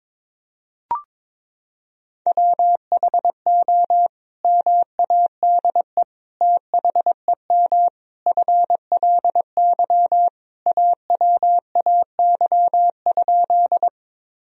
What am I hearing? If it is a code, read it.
Morse code, heard as a keyed tone, about 22 wpm, 700 Hz. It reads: WHO MADE THEM FLY AWAY?
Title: Sentences from Top 1000 Words - Review Part 2 - 22wpm